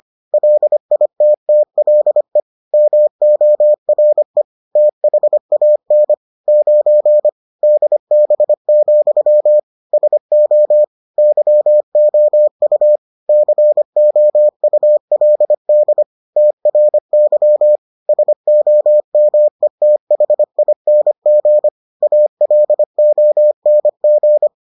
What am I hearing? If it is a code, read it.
Morse code, heard as a keyed tone, about 25 wpm, 600 Hz. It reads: LITTLE MORE THAN 9 DB, SO YOU COULD TRY SOMETHING ALONG